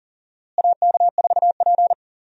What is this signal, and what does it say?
Morse code, heard as a keyed tone, about 40 wpm, 700 Hz. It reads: AK4P
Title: USA Call Signs - 4 Characters - Repeated 2x - 40wpm